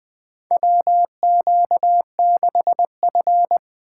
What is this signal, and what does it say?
Morse code, heard as a keyed tone, about 20 wpm, 700 Hz. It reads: WQ6F
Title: USA Call Signs - 4 Characters - 8x Character Spacing - 20wpm